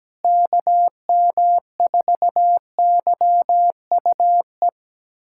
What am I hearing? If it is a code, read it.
Morse code, heard as a keyed tone, about 17 wpm, 700 Hz. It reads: KM4YUE
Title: USA Callsign DE Callsign - 6 Characters - 17wpm